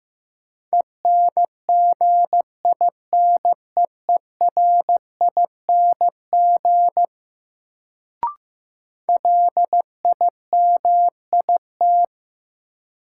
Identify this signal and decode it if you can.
Morse code, heard as a keyed tone, about 15 wpm, 700 Hz. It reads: ENGINEERING LIMIT
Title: Sentences from Top 1500 Words - New Word Drill - 15wpm